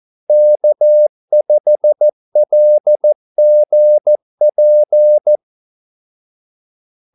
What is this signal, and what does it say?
Morse code, heard as a keyed tone, about 14 wpm, 600 Hz. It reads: K5LGP
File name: Callsigns-14wpm